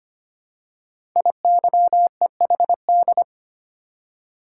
Morse code, heard as a keyed tone, about 25 wpm, 700 Hz. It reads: IYEHD
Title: Code Groups - 5 Letters - 25wpm